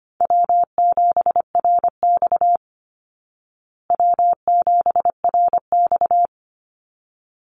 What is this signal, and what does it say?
Morse code, heard as a keyed tone, about 25 wpm, 700 Hz. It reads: W7RX W7RX
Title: Easy Call Signs Repeated 2x - 25wpm